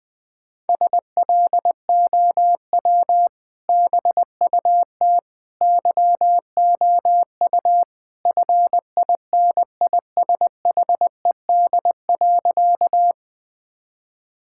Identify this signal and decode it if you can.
Morse code, heard as a keyed tone, about 19 wpm, 700 Hz. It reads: SLOW BUT YOU FINISHED.
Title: Sentences with 2-Word Phrases - Spoken Beginning - 20wpm